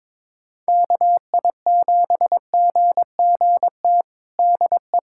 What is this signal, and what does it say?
Morse code, heard as a keyed tone, about 22 wpm, 700 Hz. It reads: KI7GGT DE
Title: USA Callsign DE Callsign - 6 Characters - 22wpm